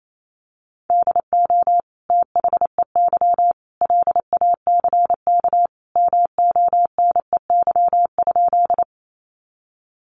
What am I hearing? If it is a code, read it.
Morse code, heard as a keyed tone, about 28 wpm, 700 Hz. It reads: DO THEY LACK MONEY?
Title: Sentences from Top 1100 Words 28wpm